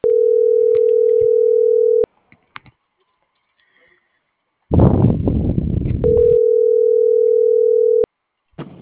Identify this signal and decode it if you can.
Ringback tone (North America), 440 + 480 Hz.